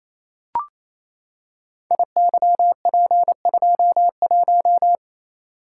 Morse code, heard as a keyed tone, about 28 wpm, 700 Hz. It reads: IYP21
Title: Code Groups - 5 Letter:Numbers - 28wpm